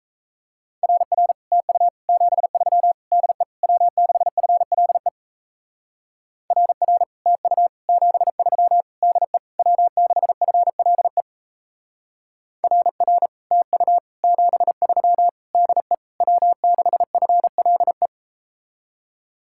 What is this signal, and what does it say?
Morse code, heard as a keyed tone, about 37 wpm, 700 Hz. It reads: RR TU 73 DE W6FLE RR TU 73 DE W6FLE RR TU 73 DE W6FLE